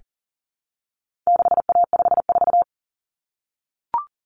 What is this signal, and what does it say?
Morse code, heard as a keyed tone, about 40 wpm, 700 Hz. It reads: 6A54
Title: Common Cut Numbers - 4-Digit 40wpm